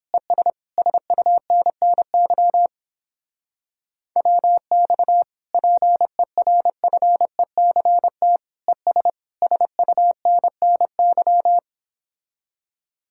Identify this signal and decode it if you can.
Morse code, heard as a keyed tone, about 31 wpm, 700 Hz. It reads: ES SUNNY WX PERFECT ES SUNNY